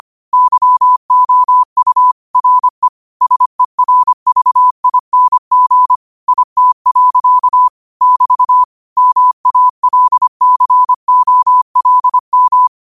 Morse code, heard as a keyed tone, about 25 wpm, 1000 Hz. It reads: YOU RE SERVING IT. <BT> MALCOLM
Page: New quotes every day in morse code at 25 Words per minute.